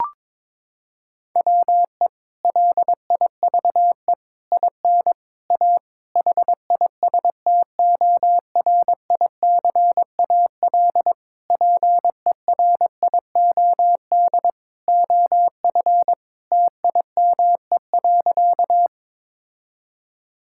Morse code, heard as a keyed tone, about 22 wpm, 700 Hz. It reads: WE LIVE IN A HISTORICAL PERIOD OF TIME.
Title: Sentences from Top 1000 Words - Review Part 5 - 22wpm